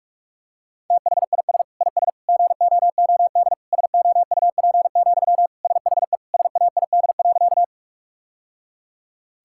Morse code, heard as a keyed tone, about 46 wpm, 700 Hz. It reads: THIS IS GOOD SOUP, SHE SAID.
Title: Sentences from Top 1000 Words - Review Part 6 - 45wpm